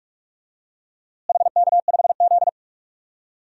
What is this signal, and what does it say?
Morse code, heard as a keyed tone, about 44 wpm, 700 Hz. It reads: HK5Z
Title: DXCC-100 Call Signs - 4 Characters - Repeated 2x - 45wpm